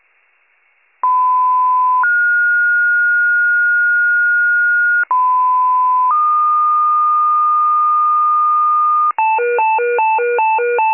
Two-tone page, heard as a one-second tone, then a three-second tone, about 1010 then 1430 Hz, about 1010 then 1220 Hz, with noise. It is heard from 1.0 to 9.1 s.